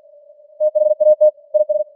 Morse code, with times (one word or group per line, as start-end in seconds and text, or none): 0.6-1.3 THAT
1.5-1.8 IS